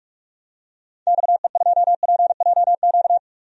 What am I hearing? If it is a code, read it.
Morse code, heard as a keyed tone, about 46 wpm, 700 Hz. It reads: XE2PJQ
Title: DXCC-100 Call Signs - 6 Characters - 45wpm